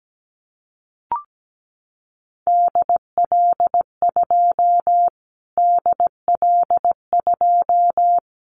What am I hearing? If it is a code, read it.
Morse code, heard as a keyed tone, about 17 wpm, 700 Hz. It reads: DL2 DL2
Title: Callsign Prefixes for Top 25 DXCC - Repeated 3x - 17wpm